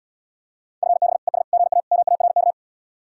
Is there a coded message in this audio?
Yes: Morse code from 0.8 to 2.5 s, heard as a keyed tone, about 80 wpm, 700 Hz. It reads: FB ES GA MARK